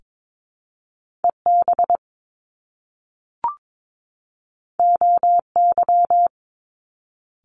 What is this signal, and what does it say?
Morse code, heard as a keyed tone, about 22 wpm, 700 Hz. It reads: EB OY
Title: Top Two Letter Combinations in Words 22wpm